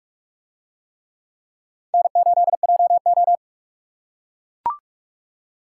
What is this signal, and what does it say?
Morse code, heard as a keyed tone, about 42 wpm, 700 Hz. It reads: N8JO